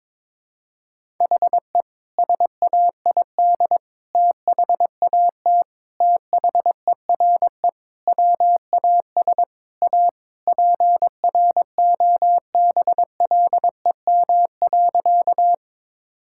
Morse code, heard as a keyed tone, about 22 wpm, 700 Hz. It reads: HE SAID THAT THERE WAS A PROBLEM.
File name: practice / 22wpm / Sentences with 3-Word Phrases - 22wpm